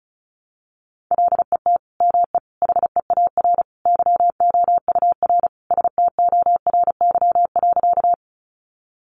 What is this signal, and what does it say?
Morse code, heard as a keyed tone, about 34 wpm, 700 Hz. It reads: LET ME HEAR YOUR STORY.
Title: Sentences from Top 300 Words 35wpm